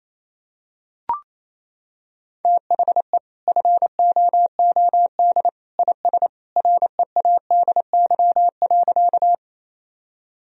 Morse code, heard as a keyed tone, about 28 wpm, 700 Hz. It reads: THE FOOD IS READY.
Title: Sentences from Top 1000 Words - Review Part 1 - 28wpm